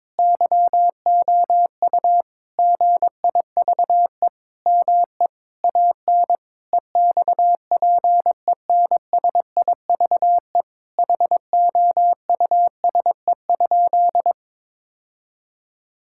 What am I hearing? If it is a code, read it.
Morse code, heard as a keyed tone, about 22 wpm, 700 Hz. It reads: YOU GIVE ME AN EXPENSIVE HOUSE?
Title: Sentences from Top 1000 Words - Review Part 6 - 22wpm